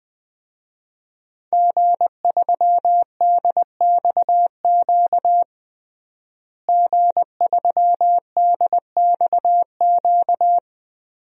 Morse code, heard as a keyed tone, about 20 wpm, 700 Hz. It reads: G3DXQ G3DXQ
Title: DXCC-100 Call Signs - 5 Characters - Repeated 3x - 20wpm